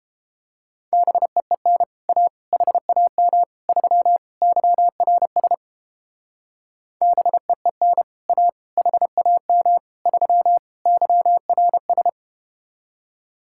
Morse code, heard as a keyed tone, about 31 wpm, 700 Hz. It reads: BEEN A HAM 3 YRS BEEN A HAM 3 YRS